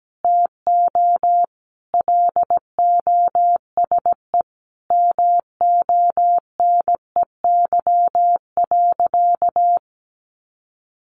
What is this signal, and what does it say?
Morse code, heard as a keyed tone, about 17 wpm, 700 Hz. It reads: TO LOSE MONEY.
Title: Sentences from Top 1100 Words 17wpm